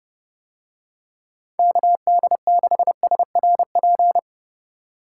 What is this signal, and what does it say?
Morse code, heard as a keyed tone, about 30 wpm, 700 Hz. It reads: KD6SRP